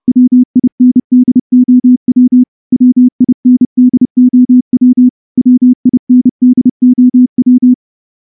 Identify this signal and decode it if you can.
Morse code, heard as a keyed tone, about 30 wpm, 250 Hz. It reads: WINDOW WINDOW WINDOW
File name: morse-training-30WPM